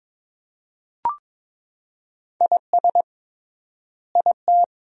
Morse code, heard as a keyed tone, about 23 wpm, 700 Hz. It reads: IS IT
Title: 3x Word-Spacing - Sentences from Top 100 Words - 22wpm